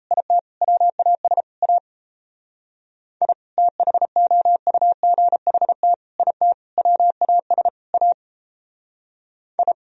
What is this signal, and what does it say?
Morse code, heard as a keyed tone, about 34 wpm, 700 Hz. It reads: IT WAS A I THOUGHT IT WAS A I
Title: Speed-Racing - Sets of 5 Words 28wpm